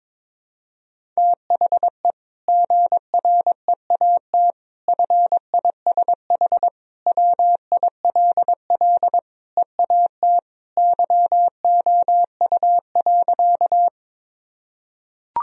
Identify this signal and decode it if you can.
Morse code, heard as a keyed tone, about 22 wpm, 700 Hz. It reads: THE GREAT FISH WILL EAT YOU.
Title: Sentences from Top 1000 Words - Review Part 1 - 22wpm